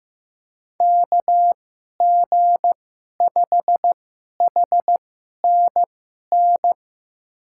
Morse code, heard as a keyed tone, about 15 wpm, spaced as for 11 wpm, 700 Hz. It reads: KG5HNN
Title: USA Call Signs - 6 Characters - 2x Character Spacing - 15wpm